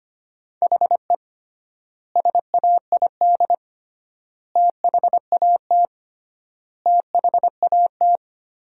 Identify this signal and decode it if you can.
Morse code, heard as a keyed tone, about 25 wpm, 700 Hz. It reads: HE SAID THAT THAT
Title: 3x Word-Spacing - Sentences from Top 200 Words - 25wpm